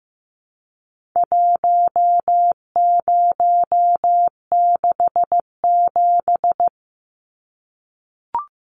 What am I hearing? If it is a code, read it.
Morse code, heard as a keyed tone, about 15 wpm, 700 Hz. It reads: 1067
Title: Numbers - 4-Digit 15wpm